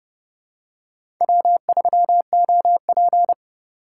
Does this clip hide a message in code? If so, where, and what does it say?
Yes: Morse code from 1.2 to 3.3 s, heard as a keyed tone, about 28 wpm, 700 Hz. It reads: W3OP